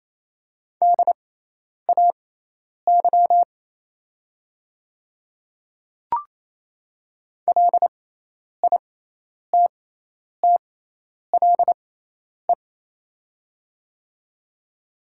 Morse code, heard as a keyed tone, about 28 wpm, spaced as for 10 wpm, 690 Hz. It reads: DAY LITTLE
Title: 6x Character Spacing - Top 300 Words - 28wpm